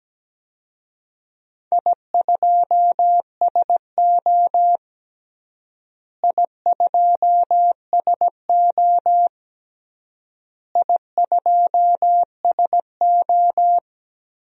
Morse code, heard as a keyed tone, about 17 wpm, 700 Hz. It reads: I2SO I2SO I2SO
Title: DXCC-100 Call Signs - 4 Characters - Repeated 3x - 17wpm